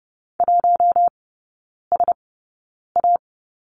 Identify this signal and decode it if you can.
Morse code, heard as a keyed tone, about 30 wpm, spaced as for 9 wpm, 700 Hz. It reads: 1SA
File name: course / 30:10wpm / 087 Lesson 15e - Number 5 - Three Characters 30wpm